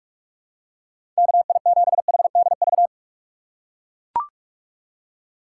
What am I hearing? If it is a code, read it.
Morse code, heard as a keyed tone, about 42 wpm, 700 Hz. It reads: KI7HDV
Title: USA Call Signs - 6 Characters - 4x Character Spacing - 45wpm